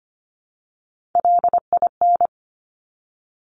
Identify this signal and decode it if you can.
Morse code, heard as a keyed tone, about 25 wpm, 700 Hz. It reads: LIN
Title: Top Three Letter Combinations in Words 25wpm